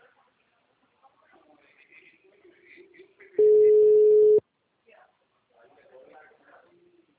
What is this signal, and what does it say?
Ringback tone (Europe, 425 Hz), with speech.